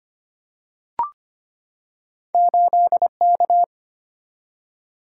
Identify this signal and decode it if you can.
Morse code, heard as a keyed tone, about 25 wpm, 700 Hz. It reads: 8K
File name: Two Letter-Number Combinations - 25wpm